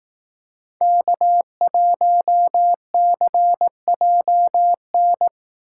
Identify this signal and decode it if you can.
Morse code, heard as a keyed tone, about 20 wpm, 700 Hz. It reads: K1CJN